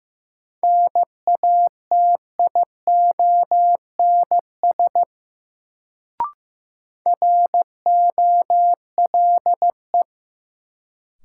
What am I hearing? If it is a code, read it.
Morse code, heard as a keyed tone, about 15 wpm, 700 Hz. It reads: NATIONS ROLE
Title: Sets of 3 Words - New Word Drill - 15wpm